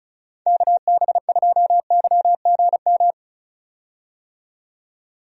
Morse code, heard as a keyed tone, about 34 wpm, 700 Hz. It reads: KB2YGM
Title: USA Call Signs - 6 Characters - 35wpm